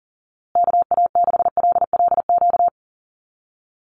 Morse code, heard as a keyed tone, about 40 wpm, 700 Hz. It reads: KA6LLQ
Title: USA Call Signs Repeated 3x 40wpm